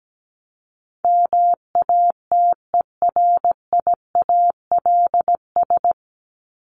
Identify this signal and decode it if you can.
Morse code, heard as a keyed tone, about 17 wpm, 700 Hz. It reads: MATERIALS
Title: ES:S Suffix - Top 100 Words with Base Words 17wpm